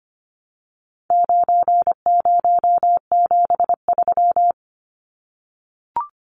Morse code, heard as a keyed tone, about 25 wpm, 700 Hz. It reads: 9073
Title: Numbers - 4-Digit 25wpm